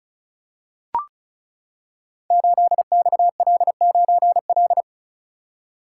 Morse code, heard as a keyed tone, about 36 wpm, 700 Hz. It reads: 8XL9L
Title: Code Groups - 5 Letter:Numbers - 35wpm